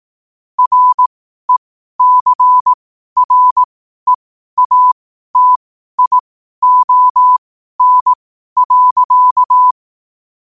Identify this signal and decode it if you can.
Morse code, heard as a keyed tone, about 18 wpm, spaced as for 12 wpm, 1000 Hz. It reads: RECREATION.
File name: qotdfiles / 12wpm